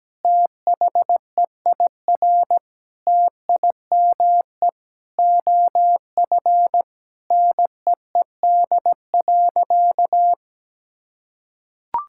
Morse code, heard as a keyed tone, about 17 wpm, 700 Hz. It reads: THEIR TIME OF NEED.
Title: Sentences from Top 1000 Words - Review Part 5 - 17wpm